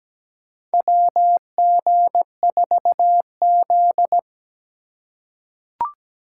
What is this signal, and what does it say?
Morse code, heard as a keyed tone, about 17 wpm, 700 Hz. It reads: WG4Z
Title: USA Call Signs - 4 Characters - 2x Character Spacing - 17wpm